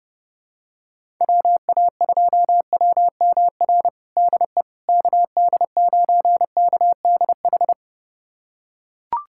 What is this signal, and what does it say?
Morse code, heard as a keyed tone, about 32 wpm, 700 Hz. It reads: WA2WMR DE KD9KDH
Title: USA Callsign DE Callsign - 6 Characters - 30wpm